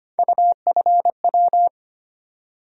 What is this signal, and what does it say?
Morse code, heard as a keyed tone, about 25 wpm, 700 Hz. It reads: UFW
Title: USA Call Sign Postfixes - 3 Letters 25wpm